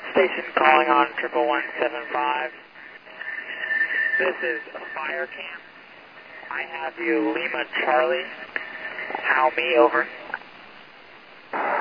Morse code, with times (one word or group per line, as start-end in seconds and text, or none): none